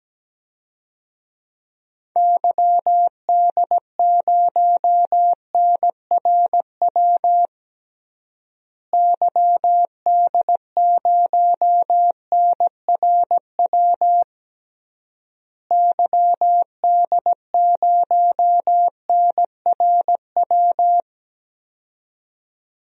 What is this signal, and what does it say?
Morse code, heard as a keyed tone, about 17 wpm, 700 Hz. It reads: YD0NRW YD0NRW YD0NRW
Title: DXCC-100 Call Signs - 6 Characters - Repeated 3x - 17wpm